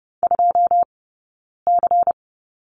Morse code, heard as a keyed tone, about 30 wpm, 700 Hz. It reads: 2 C
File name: Lesson 16d - Letter c - Two Characters 30wpm